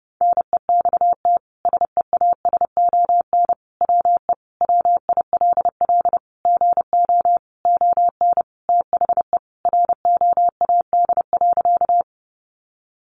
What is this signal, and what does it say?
Morse code, heard as a keyed tone, about 30 wpm, 700 Hz. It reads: NEXT SEASON WE WILL GO ON THE ROAD.
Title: Sentences from Top 1400 Words 30wpm